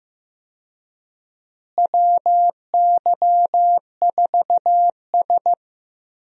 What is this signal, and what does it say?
Morse code, heard as a keyed tone, about 15 wpm, 700 Hz. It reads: WY4S
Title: USA Call Signs - 4 Characters - Repeated 2x - 15wpm